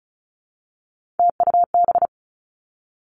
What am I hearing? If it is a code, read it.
Morse code, heard as a keyed tone, about 36 wpm, 700 Hz. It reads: TUB
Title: Three Challenging Letter-Number Combinations 35wpm